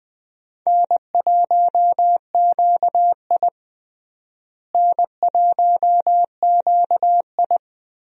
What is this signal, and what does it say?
Morse code, heard as a keyed tone, about 21 wpm, 700 Hz. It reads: N1QI N1QI
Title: USA Call Signs - 4 Characters - Repeated 3x - 20wpm